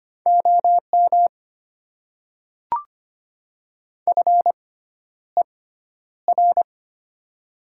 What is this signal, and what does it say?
Morse code, heard as a keyed tone, about 25 wpm, spaced as for 11 wpm, 700 Hz. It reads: OM FER
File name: Top 90 QSO Elements - 6x Character Spacing - 25wpm